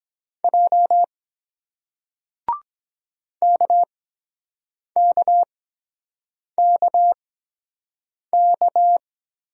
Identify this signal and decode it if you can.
Morse code, heard as a keyed tone, about 23 wpm, spaced as for 7 wpm, 700 Hz. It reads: J KKKK